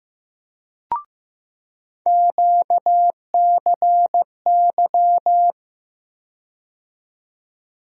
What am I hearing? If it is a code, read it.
Morse code, heard as a keyed tone, about 15 wpm, 700 Hz. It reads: QCY
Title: USA Call Sign Postfixes - 3 Letters 15wpm